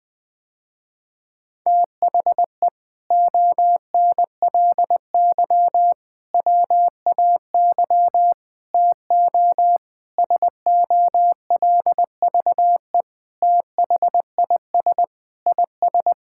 Morse code, heard as a keyed tone, about 20 wpm, 700 Hz. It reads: THE ONLY WAY TO SOLVE THIS IS
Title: Sentences with 4-Word Phrases - Spoken Beginning - 20wpm